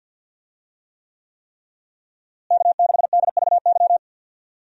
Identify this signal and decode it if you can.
Morse code, heard as a keyed tone, about 52 wpm, 700 Hz. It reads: K6DVY
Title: USA Call Signs - 5 Characters - Repeated 3x - 50wpm